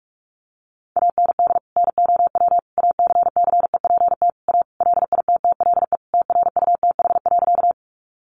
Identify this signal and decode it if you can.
Morse code, heard as a keyed tone, about 44 wpm, 700 Hz. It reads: AND NOW ACCEPT A LITTLE TRUTH.
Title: Sentences from Top 1300 Words 45wpm